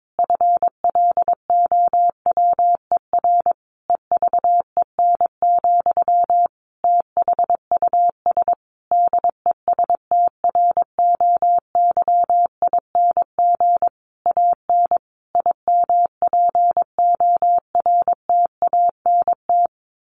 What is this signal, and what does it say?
Morse code, heard as a keyed tone, about 22 wpm, 700 Hz. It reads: FLOWER EVEN, THUS DESTROYING AN IMPORTANT
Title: A Sound Of Thunder 22wpm